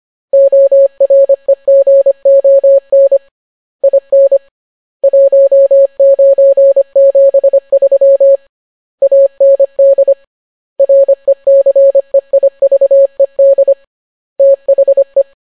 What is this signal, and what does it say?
Morse code, heard as a keyed tone, about 25 wpm, 550 Hz. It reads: OREGON IN 1973 AND RECEIVED THE